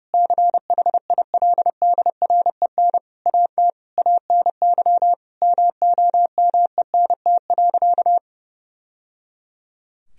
Morse code, heard as a keyed tone, about 30 wpm, 700 Hz. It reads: CHILDREN AT ANY MOMENT.
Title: Sentences from Top 1000 Words - Review Part 3 - 30wpm